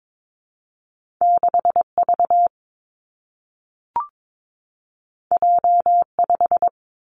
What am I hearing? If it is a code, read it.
Morse code, heard as a keyed tone, about 22 wpm, 700 Hz. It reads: 6 V J 5
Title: Two Challenging Letter-Number Combinations 22wpm